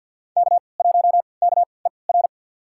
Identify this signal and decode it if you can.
Morse code, heard as a keyed tone, about 52 wpm, spaced as for 29 wpm, 700 Hz. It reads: K1KER
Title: USA Call Signs - 5 Characters - 3x Character Spacing - 50wpm